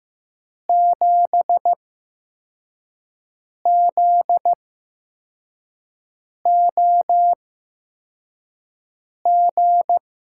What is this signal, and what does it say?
Morse code, heard as a keyed tone, about 15 wpm, spaced as for 4 wpm, 700 Hz. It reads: 7ZOG
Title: USA Call Signs - 6 Characters - 8x Character Spacing - 15wpm